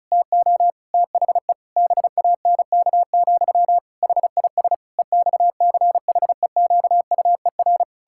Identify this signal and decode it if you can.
Morse code, heard as a keyed tone, about 34 wpm, 700 Hz. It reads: TO THE BANK, HIS EXCHEQUER
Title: Pangram Challenge v2 - 35wpm